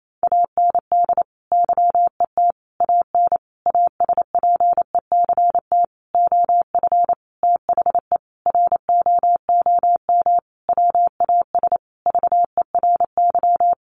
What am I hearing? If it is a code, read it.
Morse code, heard as a keyed tone, about 28 wpm, 700 Hz. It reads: AND YET AN ASPECT OF THE ROOM WAS VERY
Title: Sentences from Top 1300 Words 28wpm